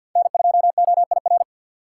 Morse code, heard as a keyed tone, about 52 wpm, 700 Hz. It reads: N2OIR
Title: USA Call Signs - 5 Characters - 3x Character Spacing - 50wpm